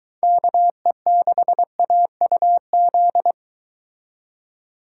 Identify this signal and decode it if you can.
Morse code, heard as a keyed tone, about 23 wpm, 700 Hz. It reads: KE6AUZ